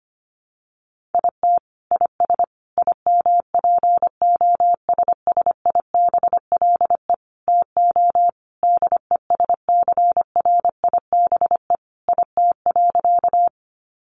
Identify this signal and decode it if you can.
Morse code, heard as a keyed tone, about 25 wpm, 700 Hz. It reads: IT IS IMPOSSIBLE TO DESCRIBE IT.
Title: Sentences from Top 1100 Words 25wpm